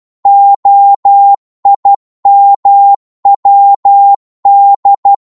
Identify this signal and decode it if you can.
Morse code, heard as a keyed tone, about 12 wpm, 800 Hz. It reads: OIMWD